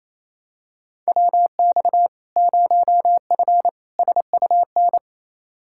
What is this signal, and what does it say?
Morse code, heard as a keyed tone, about 28 wpm, 700 Hz. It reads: WX 0F SUN
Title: Practice QSO - Exchange 2 - Info - 28wpm